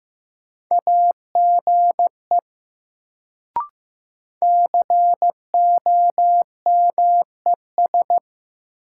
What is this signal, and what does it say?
Morse code, heard as a keyed tone, about 15 wpm, 700 Hz. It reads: AGE COMES
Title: Sets of 3 Words - New Word Drill - 15wpm